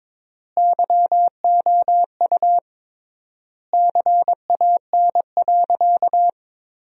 Morse code, heard as a keyed tone, about 21 wpm, 700 Hz. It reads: YOU CAN.
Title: 3x Word-Spacing - Sentences from Top 500 Words - 22wpm